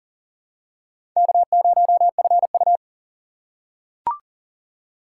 Morse code, heard as a keyed tone, about 40 wpm, 700 Hz. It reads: K0FU
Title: USA Call Signs - 4 Characters - 3x Character Spacing - 40wpm